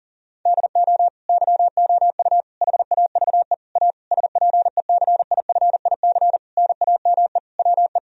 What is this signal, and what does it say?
Morse code, heard as a keyed tone, about 40 wpm, 700 Hz. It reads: DO YOU HAVE A SPECIFIC NAME WE
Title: Sentences from Top 1000 Words - Review Part 3 - 40wpm